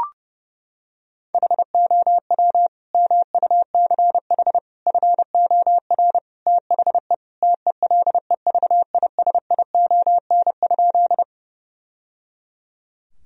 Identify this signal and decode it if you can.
Morse code, heard as a keyed tone, about 32 wpm, 700 Hz. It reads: HOW MUCH FOR THE TELEVISION?
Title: Sentences from Top 1000 Words - Review Part 4 - 30wpm